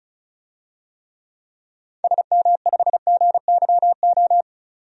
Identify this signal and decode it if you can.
Morse code, heard as a keyed tone, about 36 wpm, 700 Hz. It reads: SM5GYO
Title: DXCC-100 Call Signs - 6 Characters - 35wpm